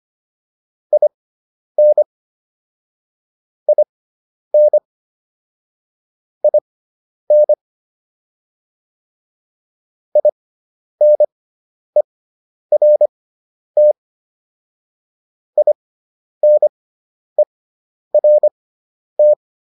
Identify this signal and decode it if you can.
Morse code, heard as a keyed tone, about 25 wpm, spaced as for 10 wpm, 600 Hz. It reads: IN IN IN INERT INERT